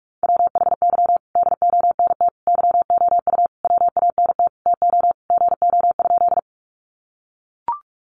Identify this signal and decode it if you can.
Morse code, heard as a keyed tone, about 44 wpm, 700 Hz. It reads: WHY DONT YOU WANT TO GO?
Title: Sentences from Top 300 Words 45wpm